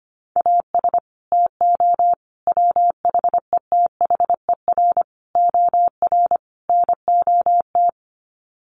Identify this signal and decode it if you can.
Morse code, heard as a keyed tone, about 25 wpm, 700 Hz. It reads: AS TO WHETHER OR NOT
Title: Sets of 5 Words Encore 25wpm